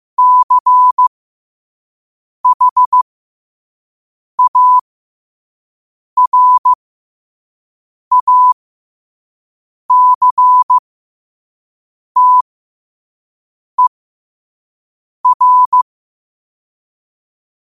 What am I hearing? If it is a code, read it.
Morse code, heard as a keyed tone, about 15 wpm, spaced as for 5 wpm, 1000 Hz. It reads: CHARACTER
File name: 5wpm